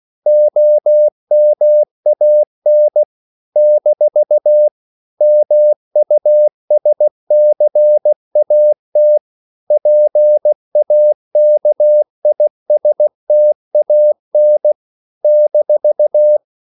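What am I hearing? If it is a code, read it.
Morse code, heard as a keyed tone, about 16 wpm, 600 Hz. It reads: OMAN - MUSCAT PAKISTAN -
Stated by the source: CW practice of national capitals at 16 words per minute, from ON6ZQ.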